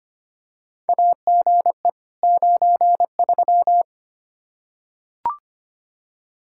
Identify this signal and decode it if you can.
Morse code, heard as a keyed tone, about 25 wpm, 700 Hz. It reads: AGE 93
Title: Practice QSO - Exchange 3 - More Info - 25wpm